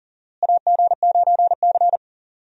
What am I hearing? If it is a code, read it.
Morse code, heard as a keyed tone, about 39 wpm, 700 Hz. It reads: AG9C
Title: USA Call Signs - 4 Characters - Repeated 2x - 40wpm